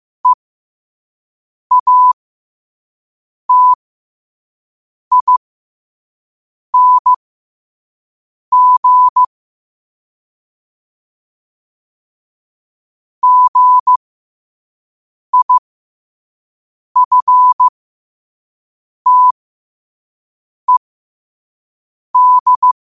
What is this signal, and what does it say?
Morse code, heard as a keyed tone, about 15 wpm, spaced as for 5 wpm, 1000 Hz. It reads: EATING GIFTED
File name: qotdfiles / 5wpm